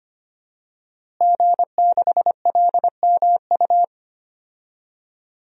Morse code, heard as a keyed tone, about 25 wpm, 700 Hz. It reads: G6LMU
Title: DXCC-100 Call Signs - 5 Characters - Repeated 3x - 25wpm